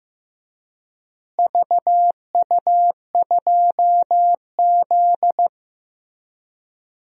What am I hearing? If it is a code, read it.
Morse code, heard as a keyed tone, about 15 wpm, 700 Hz. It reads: VU2Z
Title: DXCC-100 Call Signs - 4 Characters - Repeated 3x - 15wpm